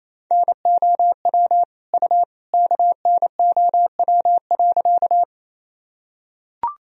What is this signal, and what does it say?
Morse code, heard as a keyed tone, about 28 wpm, 700 Hz. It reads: NOW U KNOW.
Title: Sentences from Top 100 Words and Abbreviations 28wpm